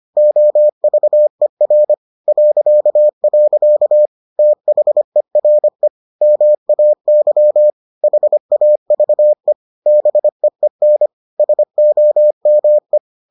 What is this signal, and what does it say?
Morse code, heard as a keyed tone, about 25 wpm, 600 Hz. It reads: OVER .. THERE MAY HAVE BEEN SOME